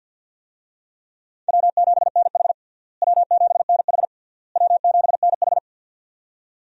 Morse code, heard as a keyed tone, about 52 wpm, 700 Hz. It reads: W7NH W7NH W7NH
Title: USA Call Signs - 4 Characters - Repeated 3x - 50wpm